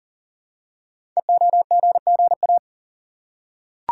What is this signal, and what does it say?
Morse code, heard as a keyed tone, about 40 wpm, 700 Hz. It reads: EOGGA
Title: Code Groups - 5 Letters - 40wpm